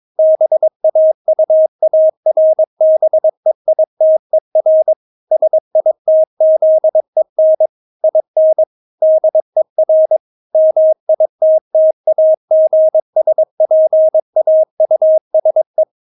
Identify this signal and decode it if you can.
Morse code, heard as a keyed tone, about 22 wpm, 630 Hz. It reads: BAUARBEITER SITZEN IN DER MITTAGSPAUSE
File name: Witz3 22wpm